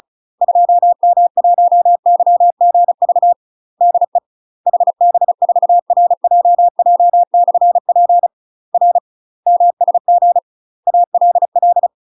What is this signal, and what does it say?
Morse code, heard as a keyed tone, about 34 wpm, 700 Hz. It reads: 2M1YGV DE HB4RJJ/P R MSG ALL
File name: UK MASTER QSOs TRAINING / 35WPM / UK MASTER QSOs 0006